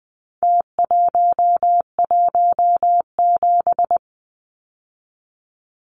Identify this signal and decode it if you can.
Morse code, heard as a keyed tone, about 20 wpm, 700 Hz. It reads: T117
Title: Numbers - Contest Style - 4-Digit 20wpm